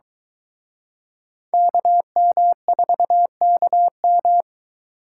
Morse code, heard as a keyed tone, about 23 wpm, 700 Hz. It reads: KM4KM